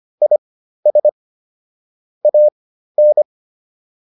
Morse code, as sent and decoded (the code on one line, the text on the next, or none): .. ... / .- -.
IS AN